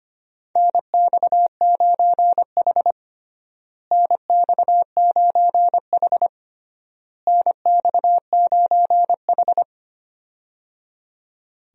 Morse code, heard as a keyed tone, about 25 wpm, 700 Hz. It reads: NX9H NX9H NX9H
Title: USA Call Signs - 4 Characters - Repeated 3x - 25wpm